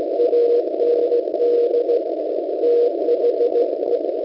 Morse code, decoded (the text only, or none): AND THE